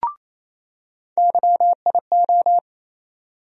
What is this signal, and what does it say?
Morse code, heard as a keyed tone, about 28 wpm, 700 Hz. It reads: YIO
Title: USA Call Sign Postfixes - 3 Letters 28wpm